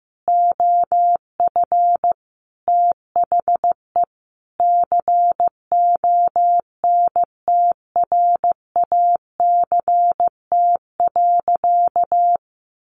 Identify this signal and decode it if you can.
Morse code, heard as a keyed tone, about 15 wpm, 700 Hz. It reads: OF THE CONTRACT.
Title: Sentences from Top 1400 Words 15wpm